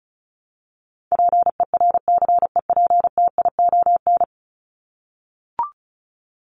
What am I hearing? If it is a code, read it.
Morse code, heard as a keyed tone, about 34 wpm, 700 Hz. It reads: PERCEPTION
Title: Top 1751-2000 Words 35wpm